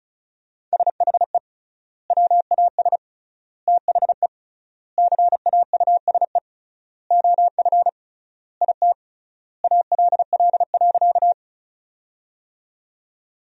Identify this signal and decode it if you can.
Morse code, heard as a keyed tone, about 34 wpm, 700 Hz. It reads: SHE WAS THE CAUSE OF IT ALL.
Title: 3x Word-Spacing - Sentences from Top 200 Words - 35wpm